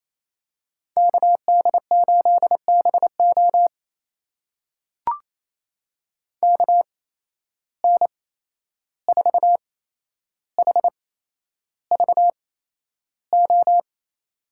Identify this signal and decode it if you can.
Morse code, heard as a keyed tone, about 28 wpm, spaced as for 11 wpm, 700 Hz. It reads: KD8BO KN4HVO
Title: USA Call Signs - 8x Character Spacing - 28wpm